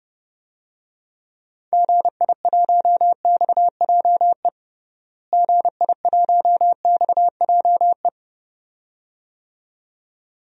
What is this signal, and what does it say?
Morse code, heard as a keyed tone, about 30 wpm, 700 Hz. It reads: GI1XJE GI1XJE
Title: DXCC-100 Call Signs - 6 Characters - Repeated 2x - 30wpm